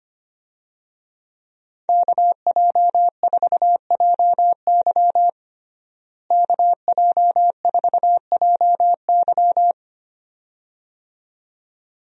Morse code, heard as a keyed tone, about 25 wpm, 700 Hz. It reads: KJ4JY KJ4JY
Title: USA Call Signs - 5 Characters - Repeated 2x - 25wpm